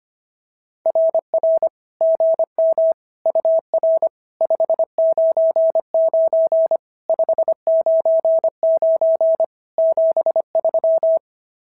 Morse code, heard as a keyed tone, about 25 wpm, 620 Hz. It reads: RR GM UR 599 599 73
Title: SOTA QSO Practice 25wpm